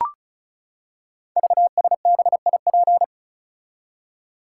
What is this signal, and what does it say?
Morse code, heard as a keyed tone, about 36 wpm, 700 Hz. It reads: VSBIP
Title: Code Groups - 5 Letter:Numbers - 35wpm